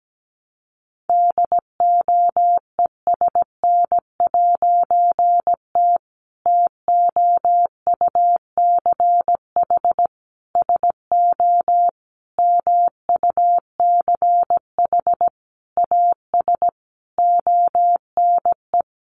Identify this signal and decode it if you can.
Morse code, heard as a keyed tone, about 17 wpm, 700 Hz. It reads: DOESN'T TOUCH SO MUCH AS ONE
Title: A Sound Of Thunder 17wpm